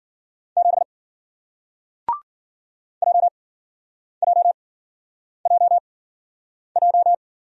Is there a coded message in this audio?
Yes: Morse code from 0.6 to 7.1 s, heard as a keyed tone, about 51 wpm, spaced as for 9 wpm, 700 Hz. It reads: 7 JJJJ